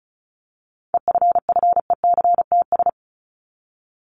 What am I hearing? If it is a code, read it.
Morse code, heard as a keyed tone, about 34 wpm, 700 Hz. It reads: EFFECTS
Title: ES:S Suffix - Top 100 Words with Base Words 35wpm